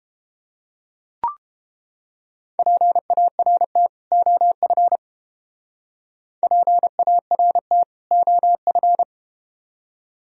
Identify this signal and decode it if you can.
Morse code, heard as a keyed tone, about 31 wpm, 700 Hz. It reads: PART OF PART OF